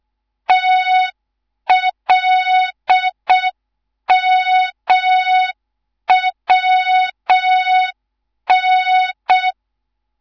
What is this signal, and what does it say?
Morse code, heard as a keyed tone, about 6 wpm, 3700 Hz. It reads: TLMWN
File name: Morze-HA2MN/hang/11 GYAKORLAT